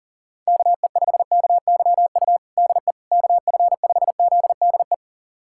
Morse code, heard as a keyed tone, about 40 wpm, 700 Hz. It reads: KE5KYU DE KF5ZDE
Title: USA Callsign DE Callsign 40wpm